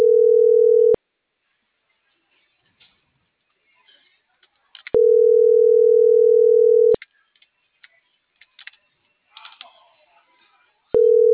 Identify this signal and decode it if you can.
Ringback tone (North America), 440 + 480 Hz.